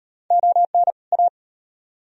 Morse code, heard as a keyed tone, about 38 wpm, 700 Hz. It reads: ON A